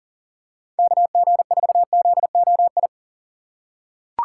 Morse code, heard as a keyed tone, about 40 wpm, 700 Hz. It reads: KG4ZOI
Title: USA Call Signs - 6 Characters - 3x Character Spacing - 40wpm